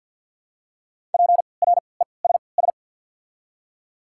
Morse code, heard as a keyed tone, about 49 wpm, spaced as for 28 wpm, 700 Hz. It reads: PRESS
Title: 3x Character Spacing - Top 500 Words - 50wpm